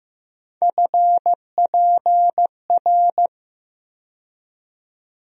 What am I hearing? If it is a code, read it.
Morse code, heard as a keyed tone, about 15 wpm, 700 Hz. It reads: FPR
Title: USA Call Sign Postfixes - 3 Letters 15wpm